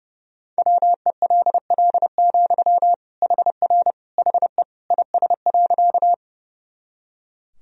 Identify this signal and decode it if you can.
Morse code, heard as a keyed tone, about 32 wpm, 700 Hz. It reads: WELL, HR HE IS.
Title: Sentences from Top 200 Words and Abbreviations 30wpm